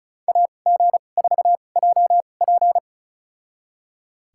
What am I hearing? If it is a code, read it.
Morse code, heard as a keyed tone, about 34 wpm, spaced as for 25 wpm, 700 Hz. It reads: AG4JP
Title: USA Call Signs - 5 Characters - 2x Character Spacing - 35wpm